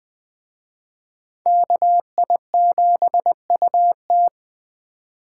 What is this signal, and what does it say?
Morse code, heard as a keyed tone, about 20 wpm, 700 Hz. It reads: KI7UT
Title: Speed Racing - USA Call Signs 17wpm